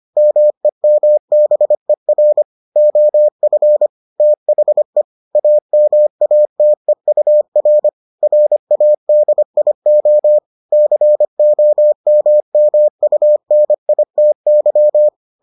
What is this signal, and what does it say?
Morse code, heard as a keyed tone, about 25 wpm, 600 Hz. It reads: MEMBER OF THE AMATEUR RADIO COMMUNITY